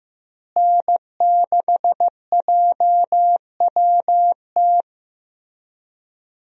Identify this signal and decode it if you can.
Morse code, heard as a keyed tone, about 15 wpm, 700 Hz. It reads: N6JWT
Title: USA Call Signs - 5 Characters - Repeated 2x - 15wpm